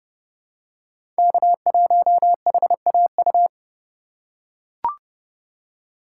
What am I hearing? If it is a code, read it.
Morse code, heard as a keyed tone, about 30 wpm, 700 Hz. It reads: K1HAU
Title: USA Call Signs - 5 Characters - 3x Character Spacing - 30wpm